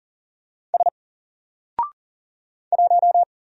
Morse code, heard as a keyed tone, about 39 wpm, 700 Hz. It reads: S 1
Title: Single Character (Letter:Number:Punctuation:Prosign) Alt 40wpm